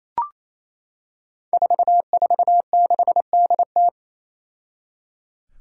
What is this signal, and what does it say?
Morse code, heard as a keyed tone, about 28 wpm, 700 Hz. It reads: 446DT
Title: Five Challenging Letter-Number Combinations 28wpm